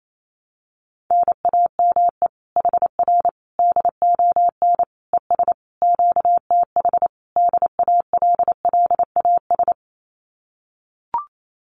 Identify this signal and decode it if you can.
Morse code, heard as a keyed tone, about 28 wpm, 700 Hz. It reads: NAME HR DON ES QTH DALLAS
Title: Common QSO Phrases Encore 28wpm